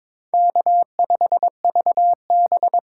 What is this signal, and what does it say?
Morse code, heard as a keyed tone, about 26 wpm, 700 Hz. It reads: K5VB
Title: Speed Racing - USA 1x2 Call Signs 22wpm